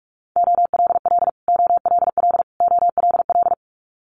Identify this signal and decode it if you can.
Morse code, heard as a keyed tone, about 46 wpm, 700 Hz. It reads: OLL OLL OLL
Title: Multi-letter Phonemes Repeated 3x 45wpm